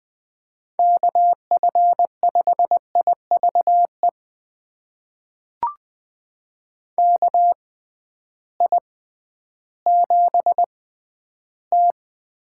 Morse code, heard as a keyed tone, about 20 wpm, spaced as for 12 wpm, 700 Hz. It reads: KF5IVE KI7T
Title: USA Call Signs - 6 Characters - 6x Character Spacing - 20wpm